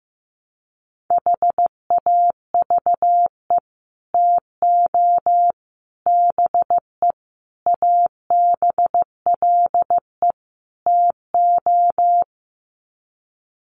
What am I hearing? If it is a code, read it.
Morse code, heard as a keyed tone, about 15 wpm, 700 Hz. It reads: HAVE TO BE ABLE TO
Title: Sets of 5 Words 15wpm